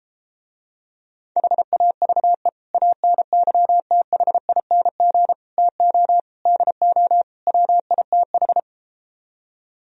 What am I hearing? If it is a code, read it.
Morse code, heard as a keyed tone, about 33 wpm, 700 Hz. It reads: HAVE ANYTHING TO DO WITH